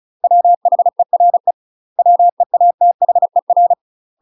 Morse code, heard as a keyed tone, about 34 wpm, 700 Hz. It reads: WHERE WEATHER
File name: UK MASTER QSOs TRAINING/35WPM